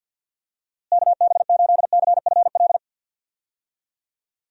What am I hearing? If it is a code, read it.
Morse code, heard as a keyed tone, about 48 wpm, 700 Hz. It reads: KB8CFL
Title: USA Call Signs - 6 Characters - Repeated 2x - 50wpm